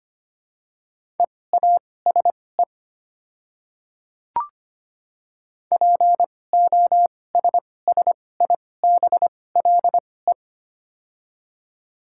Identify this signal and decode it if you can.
Morse code, heard as a keyed tone, about 25 wpm, spaced as for 18 wpm, 700 Hz. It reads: EASE POSSIBLE
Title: 2x Character Spacing - Top 500 Words - 25wpm